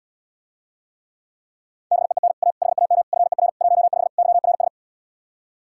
Morse code, heard as a keyed tone, about 80 wpm, 700 Hz. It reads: BEEN A HAM FER 15 YRS